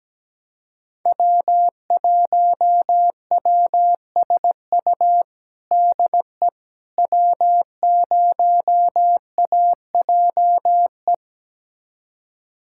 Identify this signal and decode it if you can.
Morse code, heard as a keyed tone, about 17 wpm, 700 Hz. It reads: W1WSU DE W0AJE
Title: USA Callsign DE Callsign - 5 Characters - 17wpm